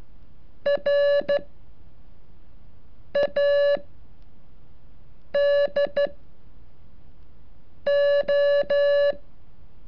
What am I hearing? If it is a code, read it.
Morse code, heard as a keyed tone, about 14 wpm, 570 Hz. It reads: R A D O
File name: curso / ejercicio 36